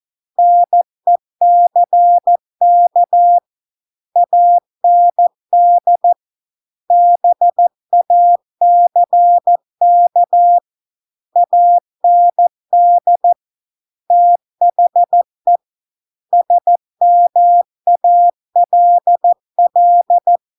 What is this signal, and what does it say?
Morse code, heard as a keyed tone, about 14 wpm, 700 Hz. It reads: NECK AND BACK AND THE SMALL